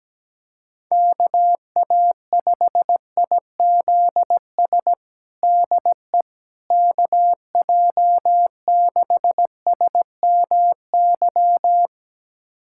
Morse code, heard as a keyed tone, about 17 wpm, 700 Hz. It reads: KA5IZS DE KJ6SMY
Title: USA Callsign DE Callsign - 6 Characters - 17wpm